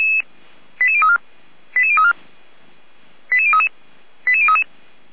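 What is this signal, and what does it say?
Five-tone selective call (ZVEI), heard as quick runs of five tones, tones of about 70 ms. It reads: 88034 88034 88033 88033
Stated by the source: ZVEI five-tone selective call received by SDR, on the feuerwehr-orlamuende fire brigade audio server.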